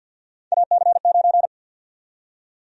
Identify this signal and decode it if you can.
Morse code, heard as a keyed tone, about 49 wpm, 700 Hz. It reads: AC9
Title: USA Call Sign Prefixes - 50wpm